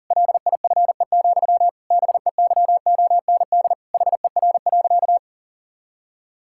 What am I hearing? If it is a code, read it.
Morse code, heard as a keyed tone, about 37 wpm, 700 Hz. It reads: LIFE, BEYOND HER.
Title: Sentences from Top 1000 Words - Review Part 6 - 40wpm